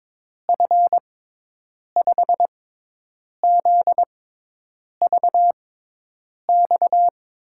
Morse code, heard as a keyed tone, about 22 wpm, spaced as for 8 wpm, 700 Hz. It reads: F5ZVX
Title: USA Call Signs - 6x Character Spacing - 22wpm